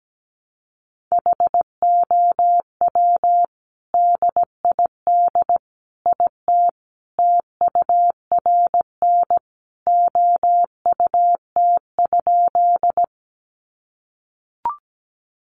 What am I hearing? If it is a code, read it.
Morse code, heard as a keyed tone, about 17 wpm, 700 Hz. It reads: HOW DID IT TURN OUT?
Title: A Sound Of Thunder 17wpm